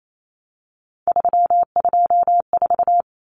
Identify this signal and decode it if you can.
Morse code, heard as a keyed tone, about 28 wpm, 700 Hz. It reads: 324
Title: Numbers - 3-Digit 28wpm